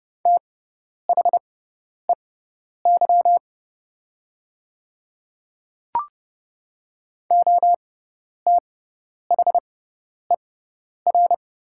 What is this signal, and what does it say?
Morse code, heard as a keyed tone, about 30 wpm, spaced as for 10 wpm, 700 Hz. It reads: THEY OTHER
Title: 6x Character Spacing - Top 200 Words - 30wpm